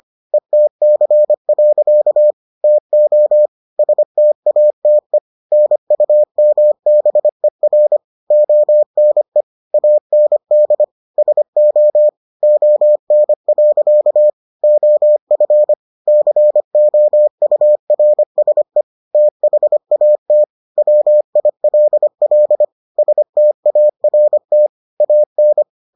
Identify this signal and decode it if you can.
Morse code, heard as a keyed tone, about 25 wpm, 600 Hz. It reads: ETC. TO STATE NUMBER ONE AND SO ON. OF COURSE THAT WILL START AN